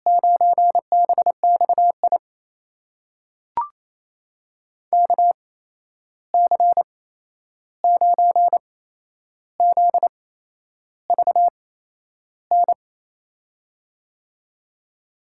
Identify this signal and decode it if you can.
Morse code, heard as a keyed tone, about 28 wpm, 700 Hz. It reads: 9 B X I K C 9 Z V N
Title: USA Call Signs - 6 Characters - 8x Character Spacing - 28wpm